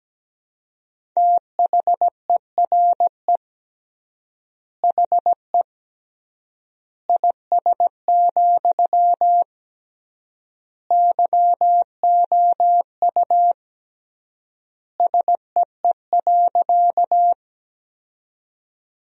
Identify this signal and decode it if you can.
Morse code, heard as a keyed tone, about 17 wpm, 700 Hz. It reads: THERE HE IS, YOU SEE.
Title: 3x Word-Spacing - Sentences from Top 100 Words - 17wpm